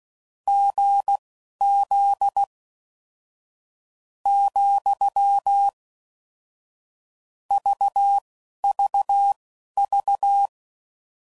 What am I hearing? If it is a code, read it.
Morse code, heard as a keyed tone, about 16 wpm, spaced as for 11 wpm, 800 Hz. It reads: GZ , VVV
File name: Combinations / Difficult Combos / GZ GZ7